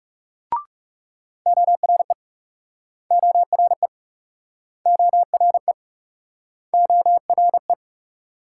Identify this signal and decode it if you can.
Morse code, heard as a keyed tone, about 38 wpm, 700 Hz. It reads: ORE ORE ORE ORE